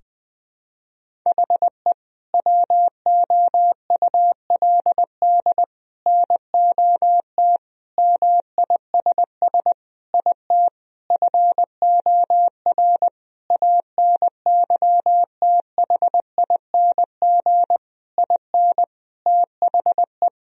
Morse code, heard as a keyed tone, about 20 wpm, 700 Hz. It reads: HE WOULD NOT MISS IT FOR ANYTHING IN THE
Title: Sentences from Top 1000 Words - Review Part 3 - 20wpm